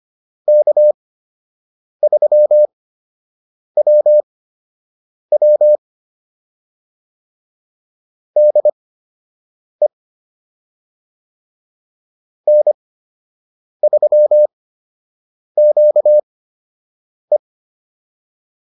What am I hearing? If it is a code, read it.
Morse code, heard as a keyed tone, about 25 wpm, spaced as for 7 wpm, 600 Hz. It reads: K3WW DE N3QE